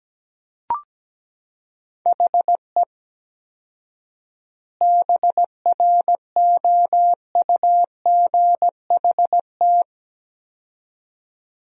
Morse code, heard as a keyed tone, about 17 wpm, 700 Hz. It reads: HE BROUGHT
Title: 4x Word-Spacing - Sentences from Top 500 Words - 17wpm